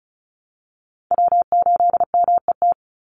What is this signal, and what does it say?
Morse code, heard as a keyed tone, about 34 wpm, 700 Hz. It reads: W8MET
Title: Call Signs Repeated 2x - 35wpm